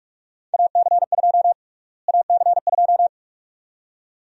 Morse code, heard as a keyed tone, about 45 wpm, 700 Hz. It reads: AC2 AC2
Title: USA Call Sign Prefixes - Repeated 2x - 45wpm